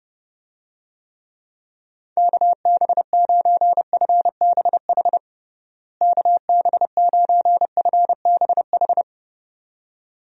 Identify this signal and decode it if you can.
Morse code, heard as a keyed tone, about 32 wpm, 700 Hz. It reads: KB9FBH KB9FBH
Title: USA Call Signs - 6 Characters - Repeated 2x - 30wpm